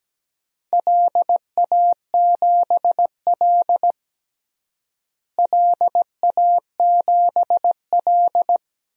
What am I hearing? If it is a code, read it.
Morse code, heard as a keyed tone, about 17 wpm, 700 Hz. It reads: LA7L LA7L
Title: DXCC-100 Call Signs - 4 Characters - Repeated 2x - 17wpm